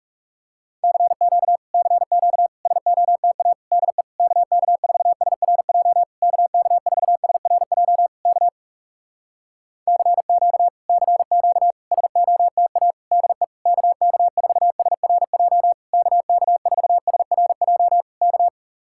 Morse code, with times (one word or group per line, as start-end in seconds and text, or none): 0.8-1.6 CQ
1.7-2.5 CQ
2.7-3.5 SOTA
3.7-4.0 DE
4.2-6.0 KK4SRJ
6.2-8.1 KK4SRJ
8.3-8.5 K
9.9-10.7 CQ
10.9-11.7 CQ
11.9-12.9 SOTA
13.1-13.4 DE
13.7-15.7 KK4SRJ
15.9-18.0 KK4SRJ
18.2-18.5 K